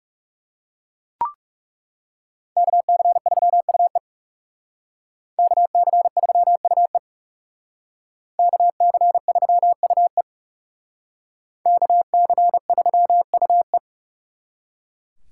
Morse code, heard as a keyed tone, about 38 wpm, 700 Hz. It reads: KC3UE KC3UE KC3UE KC3UE